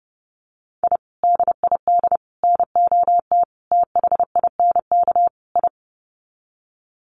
Morse code, heard as a keyed tone, about 30 wpm, 700 Hz. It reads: I DID NOT THINK I
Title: Sets of 5 Words Encore 30wpm